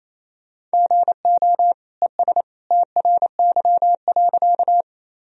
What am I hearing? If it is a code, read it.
Morse code, heard as a keyed tone, about 28 wpm, 700 Hz. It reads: GO ES TRY.
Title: Sentences from Top 200 Words and Abbreviations 28wpm